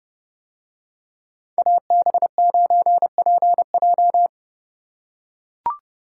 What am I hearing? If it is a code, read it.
Morse code, heard as a keyed tone, about 29 wpm, 700 Hz. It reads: AB9PJ